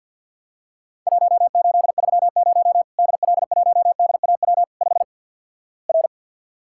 Morse code, heard as a keyed tone, about 50 wpm, 700 Hz. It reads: 1830 DL1DAW 5 R
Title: Worked All Europe DX Contest - QTC Exchange - 50wpm